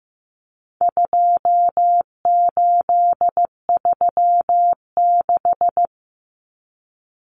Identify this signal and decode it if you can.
Morse code, heard as a keyed tone, about 15 wpm, 700 Hz. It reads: 2836
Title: Common Cut Numbers - 4-Digit 15wpm